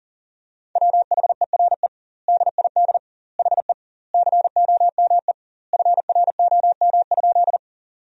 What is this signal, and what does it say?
Morse code, heard as a keyed tone, about 40 wpm, 700 Hz. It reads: WHERE DID HE COME FROM?
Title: 2x Word-Spacing - Sentences from Top 200 Words - 40wpm